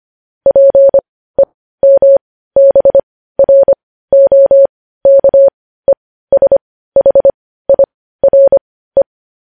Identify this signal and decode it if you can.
Morse code, heard as a keyed tone, about 25 wpm, spaced as for 15 wpm, 550 Hz. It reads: PEMBROKESHIRE
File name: podcast / News headlines-1x-25x15